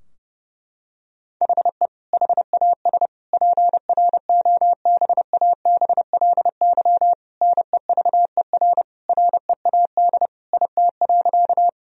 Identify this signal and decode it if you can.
Morse code, heard as a keyed tone, about 30 wpm, 700 Hz. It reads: HE HAS PROBABLY NEVER READ IT.
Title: Sentences from Top 1000 Words - Review Part 4 - 30wpm